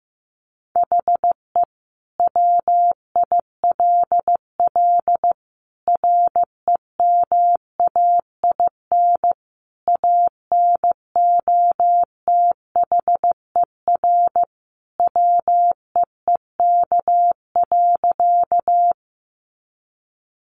Morse code, heard as a keyed tone, about 15 wpm, 700 Hz. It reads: HE WILL REMAIN ANOTHER WEEK.
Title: Sentences from Top 1200 Words 15wpm